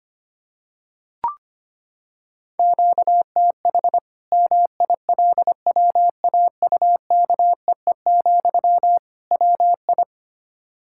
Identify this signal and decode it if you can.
Morse code, heard as a keyed tone, about 25 wpm, 700 Hz. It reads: QTH MILWAUKEE, WI
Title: Practice QSO - Exchange 1 - Beginning - 25wpm